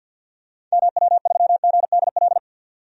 Morse code, heard as a keyed tone, about 47 wpm, 700 Hz. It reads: MW3GDL
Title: DXCC-100 Call Signs - 6 Characters - 50wpm